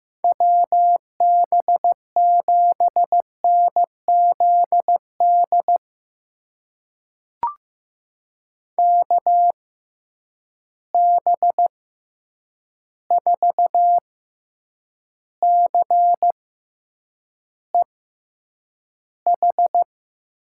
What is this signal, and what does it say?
Morse code, heard as a keyed tone, about 15 wpm, spaced as for 8 wpm, 700 Hz. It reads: WB7NZD KB4CEH
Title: USA Call Signs - 6 Characters - 6x Character Spacing - 15wpm